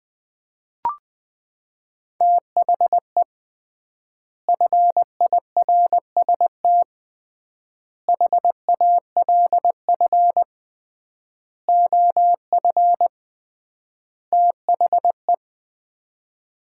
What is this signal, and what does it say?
Morse code, heard as a keyed tone, about 20 wpm, 700 Hz. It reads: THE FIRST HALF OF THE
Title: 3x Word-Spacing - Sets of 5 Words - 20wpm